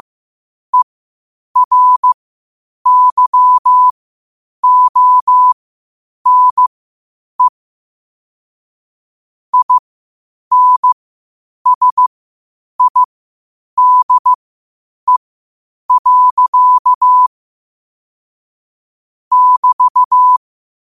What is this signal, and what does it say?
Morse code, heard as a keyed tone, about 15 wpm, spaced as for 8 wpm, 1000 Hz. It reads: ERYONE INSIDE. <BT>
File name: qotdfiles / 8wpm